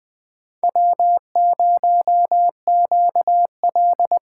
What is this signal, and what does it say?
Morse code, heard as a keyed tone, about 21 wpm, 700 Hz. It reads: W0QL
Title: Speed Racing - USA 1x2 Call Signs 17wpm